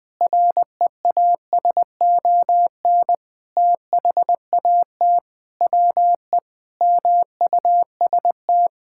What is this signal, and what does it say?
Morse code, heard as a keyed tone, about 20 wpm, 700 Hz. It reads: REASON THAT WE MUST
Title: Sentences with 6-Word Phrases - 20wpm